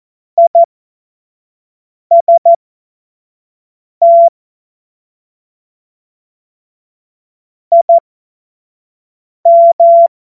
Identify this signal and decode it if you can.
Morse code, heard as a keyed tone, about 14 wpm, spaced as for 5 wpm, 680 Hz. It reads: IST IM
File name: Morsekurs / Witze / Witz 15. November 2022 25 F70